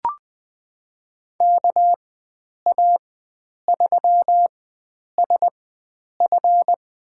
Morse code, heard as a keyed tone, about 20 wpm, spaced as for 9 wpm, 700 Hz. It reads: KA3SF
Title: USA Call Signs - 4x Character Spacing - 20wpm